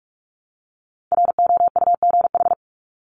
Morse code, heard as a keyed tone, about 44 wpm, 700 Hz. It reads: ROUGH
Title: Top 1501-1750 Words 45wpm